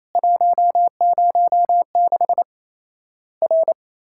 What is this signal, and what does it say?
Morse code, heard as a keyed tone, about 27 wpm, 700 Hz. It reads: 106 R
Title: Worked All Europe DX Contest - QTC Exchange - 28wpm